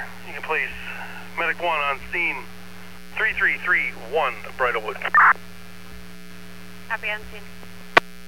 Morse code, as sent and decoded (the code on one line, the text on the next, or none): none